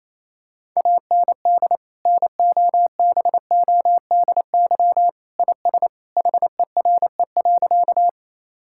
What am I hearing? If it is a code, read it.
Morse code, heard as a keyed tone, about 28 wpm, 700 Hz. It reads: AND NOBODY IS HERE.
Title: Sentences from Top 1600 Words 28wpm